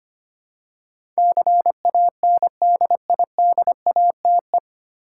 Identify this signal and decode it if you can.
Morse code, heard as a keyed tone, about 25 wpm, 700 Hz. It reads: CANDIDATE
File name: practice / 25wpm / 9 Letter Words - First 3 Letters Given - 25wpm